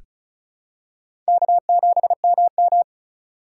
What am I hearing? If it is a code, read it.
Morse code, heard as a keyed tone, about 36 wpm, 700 Hz. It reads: K7MM
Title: USA Call Signs - 4 Characters - 2x Character Spacing - 35wpm